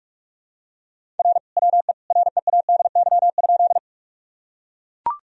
Morse code, heard as a keyed tone, about 45 wpm, 700 Hz. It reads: R WE READY?
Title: Sentences from Top 400 Words and Abbreviations 45wpm